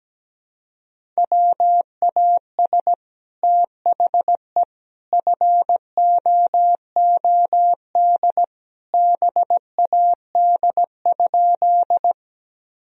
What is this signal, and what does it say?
Morse code, heard as a keyed tone, about 17 wpm, 700 Hz. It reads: WAS THE FOOD BAD?
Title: Sentences from Top 1000 Words - Review Part 1 - 17wpm